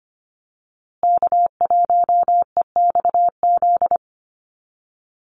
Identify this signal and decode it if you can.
Morse code, heard as a keyed tone, about 25 wpm, 700 Hz. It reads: K1EXZ
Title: USA Call Signs Repeated 2x 25wpm